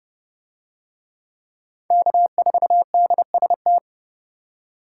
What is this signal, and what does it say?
Morse code, heard as a keyed tone, about 28 wpm, 700 Hz. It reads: K4DST